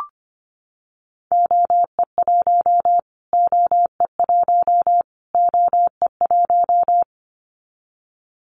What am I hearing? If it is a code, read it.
Morse code, heard as a keyed tone, about 25 wpm, 700 Hz. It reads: OE1 OE1 OE1
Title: Callsign Prefixes for Top 25 DXCC - Repeated 3x - 25wpm